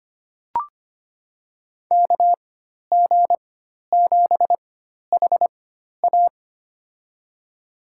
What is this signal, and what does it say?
Morse code, heard as a keyed tone, about 25 wpm, spaced as for 12 wpm, 700 Hz. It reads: KG7HA
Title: USA Call Signs - 5 Characters - 4x Character Spacing - 25wpm